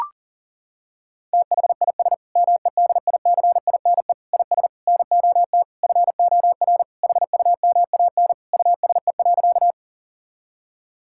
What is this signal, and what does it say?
Morse code, heard as a keyed tone, about 40 wpm, 690 Hz. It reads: THIS MEDICINE IS NOT FOR HUMAN USE.
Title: Sentences from Top 1000 Words - Review Part 6 - 40wpm